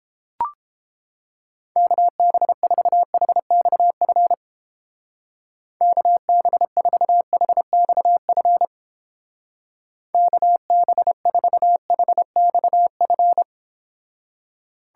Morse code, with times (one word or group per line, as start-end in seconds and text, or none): 1.8-4.3 KB4HXF
5.8-8.7 KB4HXF
10.1-13.4 KB4HXF